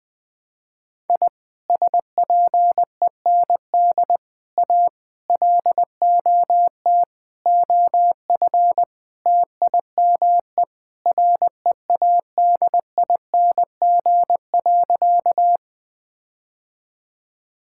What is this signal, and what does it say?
Morse code, heard as a keyed tone, about 20 wpm, 700 Hz. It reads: I SPEND A LOT OF TIME READING.
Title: Sentences with 5-Word Phrases - 20wpm